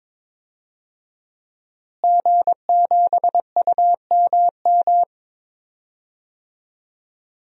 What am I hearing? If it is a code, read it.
Morse code, heard as a keyed tone, about 21 wpm, 700 Hz. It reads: G7UMM
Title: DXCC-100 Call Signs - 5 Characters - Repeated 2x - 22wpm